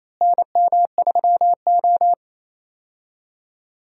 Morse code, heard as a keyed tone, about 28 wpm, 700 Hz. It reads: NM3O
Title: USA Call Signs - 4 Characters - Repeated 3x - 28wpm